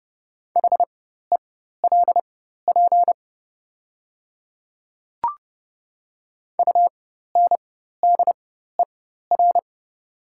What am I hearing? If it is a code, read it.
Morse code, heard as a keyed tone, about 30 wpm, spaced as for 14 wpm, 700 Hz. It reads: HELP UNDER
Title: 4x Character Spacing - Top 300 Words - 30wpm